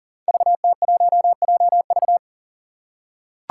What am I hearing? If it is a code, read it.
Morse code, heard as a keyed tone, about 40 wpm, 700 Hz. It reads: VT1JV
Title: Five Challenging Letter-Number Combinations 40wpm